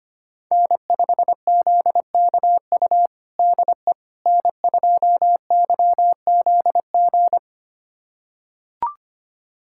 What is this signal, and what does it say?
Morse code, heard as a keyed tone, about 25 wpm, 700 Hz. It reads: N5ZKU DE N2YZG
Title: USA Callsign DE Callsign - 5 Characters - 25wpm